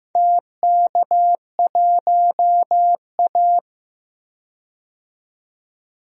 Morse code, heard as a keyed tone, about 15 wpm, 700 Hz. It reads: TK1A
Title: DXCC-100 Call Signs - 4 Characters - 15wpm